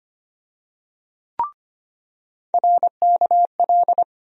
Morse code, heard as a keyed tone, about 25 wpm, 700 Hz. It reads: RKL
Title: USA Call Sign Postfixes - 3 Letters 25wpm